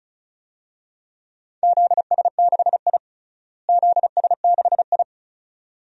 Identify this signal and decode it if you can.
Morse code, heard as a keyed tone, about 36 wpm, 700 Hz. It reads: ZS6I ZS6I
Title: DXCC-100 Call Signs - 4 Characters - Repeated 2x - 35wpm